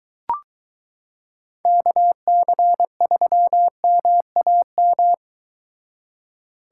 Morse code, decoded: KC3MAM

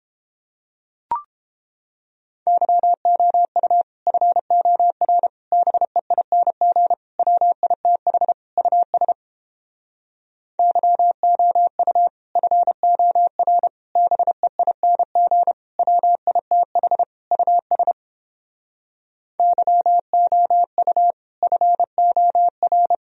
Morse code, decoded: YOU FOR BEING WITH US YOU FOR BEING WITH US YOU FOR

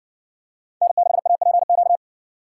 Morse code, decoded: N6APQ

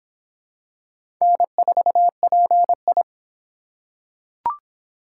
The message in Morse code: N4PI

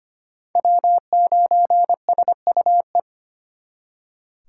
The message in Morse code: W9SUE

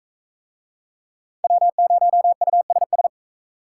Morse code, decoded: W0USS

